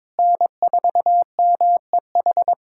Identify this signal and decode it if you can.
Morse code, heard as a keyed tone, about 22 wpm, 700 Hz. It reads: N4MEH
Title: USA Callsign DE Callsign - 5 Characters - 22wpm